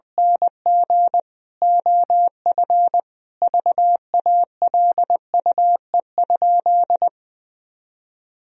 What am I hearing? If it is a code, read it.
Morse code, heard as a keyed tone, about 20 wpm, 700 Hz. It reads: NG OF VALUE?
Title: Sentences from Top 1000 Words - Review Part 3 - 20wpm